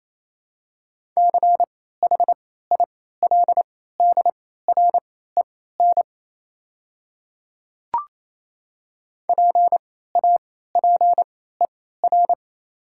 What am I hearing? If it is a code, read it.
Morse code, heard as a keyed tone, about 28 wpm, spaced as for 16 wpm, 700 Hz. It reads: CHILDREN PAPER
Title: 3x Character Spacing - Top 300 Words - 28wpm